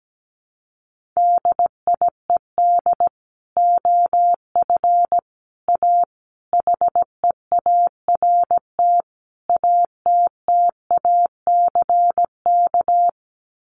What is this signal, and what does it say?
Morse code, heard as a keyed tone, about 17 wpm, 700 Hz. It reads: DIED OF A HEART ATTACK
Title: Sets of 5 Words Encore2 17wpm